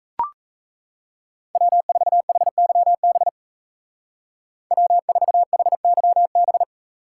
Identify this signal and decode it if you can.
Morse code, heard as a keyed tone, about 34 wpm, 700 Hz. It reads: W4HYB W4HYB
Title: Speed Racing - USA 1x3 Call Signs 28wpm